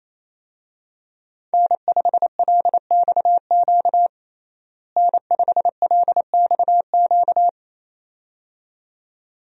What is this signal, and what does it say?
Morse code, heard as a keyed tone, about 28 wpm, 700 Hz. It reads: N5LXQ N5LXQ
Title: USA Call Signs - 5 Characters - Repeated 2x - 28wpm